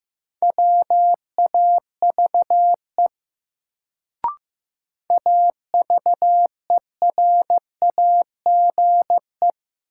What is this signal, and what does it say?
Morse code, heard as a keyed tone, about 15 wpm, 700 Hz. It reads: WAVE AVERAGE
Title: Sentences from Top 1400 Words - New Word Drill Alt - 15wpm